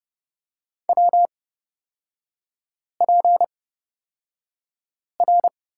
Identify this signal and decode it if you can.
Morse code, heard as a keyed tone, about 30 wpm, spaced as for 5 wpm, 700 Hz. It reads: WPR
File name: Lesson 37e - Letter x - Three Characters 30wpm